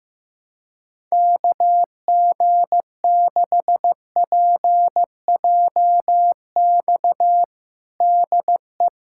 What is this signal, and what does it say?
Morse code, heard as a keyed tone, about 15 wpm, 700 Hz. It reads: KG6PJX DE
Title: USA Callsign DE Callsign - 6 Characters - 15wpm